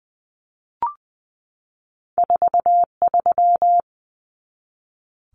Morse code, heard as a keyed tone, about 20 wpm, 700 Hz. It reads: 43
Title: Common Cut Numbers - 2-Digit 20wpm